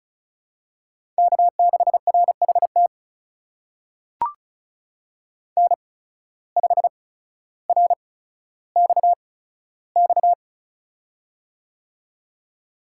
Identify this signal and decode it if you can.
Morse code, heard as a keyed tone, about 36 wpm, spaced as for 15 wpm, 700 Hz. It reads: K6RHT N5RXX
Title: USA Call Signs - 5 Characters - 8x Character Spacing - 35wpm